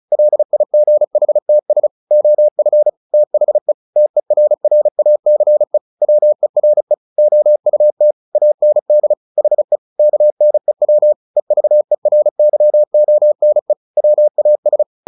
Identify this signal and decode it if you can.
Morse code, heard as a keyed tone, about 35 wpm, 600 Hz. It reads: LIGHTS OF THE TERRACE WERE OUT AND HE KNEW EVERYONE WAS
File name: the old man and the sea 2 35wpm